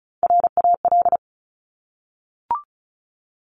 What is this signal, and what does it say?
Morse code, heard as a keyed tone, about 36 wpm, 700 Hz. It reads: RAL
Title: Top Three Letter Combinations in Words 35wpm